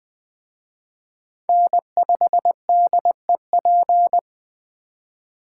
Morse code, heard as a keyed tone, about 20 wpm, 700 Hz. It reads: N5DEP